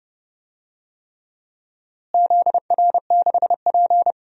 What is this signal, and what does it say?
Morse code, heard as a keyed tone, about 30 wpm, 700 Hz. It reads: ZR6P
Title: DXCC-100 Call Signs - 4 Characters - Repeated 3x - 30wpm